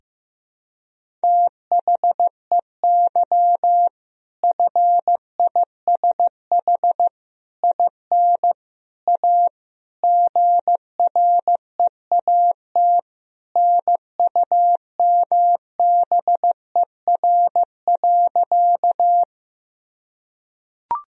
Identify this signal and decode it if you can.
Morse code, heard as a keyed tone, about 15 wpm, 700 Hz. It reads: THEY FISH IN A GREAT NUMBER.
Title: Sentences from Top 1000 Words - Review Part 1 - 15wpm